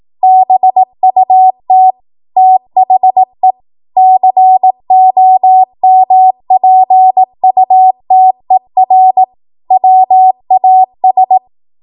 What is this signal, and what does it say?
Morse code, heard as a keyed tone, about 18 wpm, 750 Hz. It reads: BUT THE COMPUTER WAS
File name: Morse / 18WPM